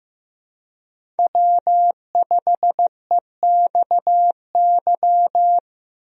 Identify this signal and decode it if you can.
Morse code, heard as a keyed tone, about 15 wpm, 700 Hz. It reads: W5EXY
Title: USA Call Signs - 5 Characters - Repeated 2x - 15wpm